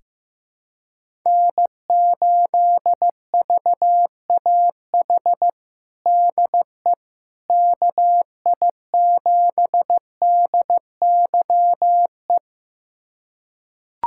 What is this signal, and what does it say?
Morse code, heard as a keyed tone, about 15 wpm, 700 Hz. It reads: N8VAH DE KI7DYE
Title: USA Callsign DE Callsign 15wpm